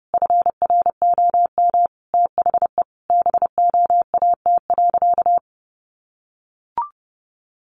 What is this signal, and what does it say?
Morse code, heard as a keyed tone, about 30 wpm, 700 Hz. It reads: FROM THE BOAT.
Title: Sentences from Top 1400 Words 30wpm